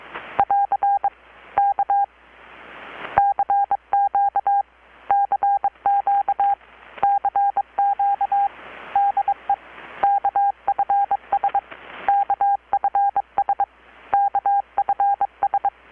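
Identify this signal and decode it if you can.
Morse code, heard as a keyed tone, about 23 wpm, 800 Hz. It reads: <AR> K CQ CQ CQ DE KFS KFS KFS
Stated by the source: Morse from KPH's Night of Nights XX maritime radiotelegraphy event, received in Chiapas, Mexico about 3,800 km away on a Kenwood R-600 with a grounded 10 m long-wire antenna.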